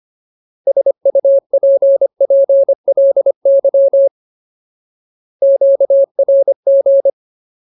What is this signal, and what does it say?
Morse code, heard as a keyed tone, about 25 wpm, 550 Hz. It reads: SUPPLY QRG
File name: Ultimate Word List / 25WPM Ultimate